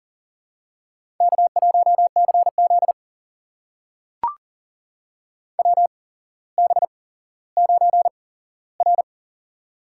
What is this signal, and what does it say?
Morse code, heard as a keyed tone, about 42 wpm, spaced as for 17 wpm, 700 Hz. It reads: K1CZ WB9R
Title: USA Call Signs - 4 Characters - 8x Character Spacing - 40wpm